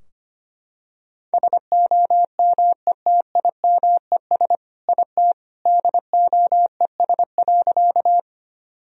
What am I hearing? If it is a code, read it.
Morse code, heard as a keyed tone, about 25 wpm, 700 Hz. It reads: SOMETIMES IT DOES.
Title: Sentences from Top 1000 Words - Review Part 3 - 25wpm